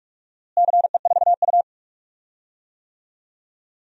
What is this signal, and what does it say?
Morse code, heard as a keyed tone, about 46 wpm, 700 Hz. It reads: CE4U
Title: DXCC-100 Call Signs - 4 Characters - Repeated 2x - 45wpm